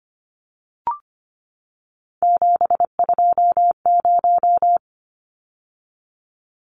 Morse code, heard as a keyed tone, about 25 wpm, 700 Hz. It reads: 720
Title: Numbers - 3-Digit 25wpm